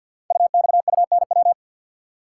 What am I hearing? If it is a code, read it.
Morse code, heard as a keyed tone, about 48 wpm, 700 Hz. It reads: UXUNW
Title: Code Groups - 5 Letters - 50wpm